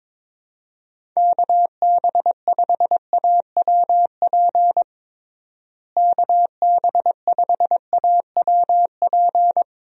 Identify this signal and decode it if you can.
Morse code, heard as a keyed tone, about 22 wpm, 700 Hz. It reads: KB5AWP KB5AWP
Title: USA Call Signs - 6 Characters - Repeated 2x - 22wpm